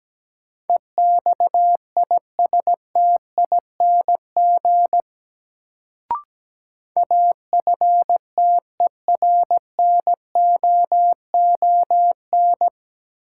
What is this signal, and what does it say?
Morse code, heard as a keyed tone, about 17 wpm, 700 Hz. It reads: EXISTING AFTERNOON
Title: Sentences from Top 1500 Words - New Word Drill - 17wpm